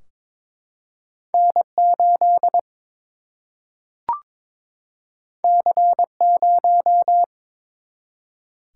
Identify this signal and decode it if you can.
Morse code, heard as a keyed tone, about 21 wpm, 700 Hz. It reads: N 8 C 0
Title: Single-Letter Single-Number 22wpm